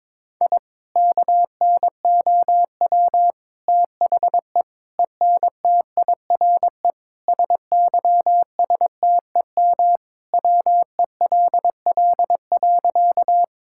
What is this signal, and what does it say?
Morse code, heard as a keyed tone, about 22 wpm, 700 Hz. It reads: I KNOW THE ENTIRE SYSTEM WELL.
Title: Sentences from Top 1000 Words - Review Part 5 - 22wpm